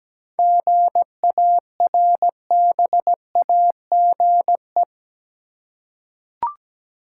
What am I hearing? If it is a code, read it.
Morse code, heard as a keyed tone, about 17 wpm, 700 Hz. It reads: GARBAGE
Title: Seven Letter Words - 17wpm